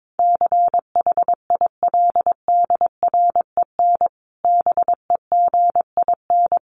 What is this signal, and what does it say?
Morse code, heard as a keyed tone, about 22 wpm, 700 Hz. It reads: CHILDREN BEGIN
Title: Sentences from Top 1500 Words 22wpm